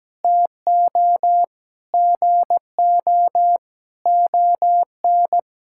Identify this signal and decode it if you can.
Morse code, heard as a keyed tone, about 17 wpm, 700 Hz. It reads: TO GO ON
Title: Sentences from Top 1000 Words - Review Part 5 - 17wpm